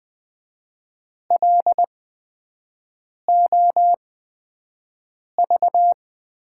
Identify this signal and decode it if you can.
Morse code, heard as a keyed tone, about 20 wpm, spaced as for 5 wpm, 700 Hz. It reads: LOV
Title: USA Call Signs - 8x Character Spacing - 20wpm